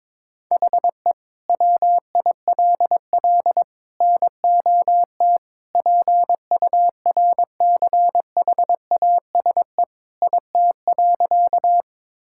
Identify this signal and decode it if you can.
Morse code, heard as a keyed tone, about 22 wpm, 700 Hz. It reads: HE WILL NOT PURCHASE IT.
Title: Sentences from Top 1000 Words - Review Part 5 - 22wpm